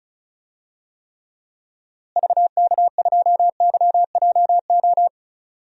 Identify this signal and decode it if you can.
Morse code, heard as a keyed tone, about 34 wpm, 700 Hz. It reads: VK2YJO
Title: DXCC-100 Call Signs - 6 Characters - Repeated 2x - 35wpm